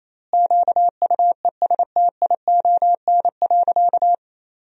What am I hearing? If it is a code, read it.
Morse code, heard as a keyed tone, about 28 wpm, 700 Hz. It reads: QUESTION.
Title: 3x Word-Spacing - Sentences from Top 400 Words - 28wpm